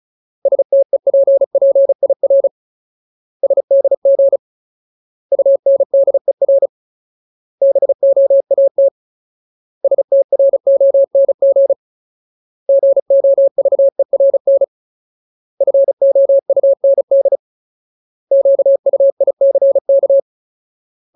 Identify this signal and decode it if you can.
Morse code, heard as a keyed tone, about 34 wpm, 550 Hz. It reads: STEPPIR SDG UNDER BOAT STRONG GOVERN FOUND QUICK
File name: Ultimate Word List / 35WPM Ultimate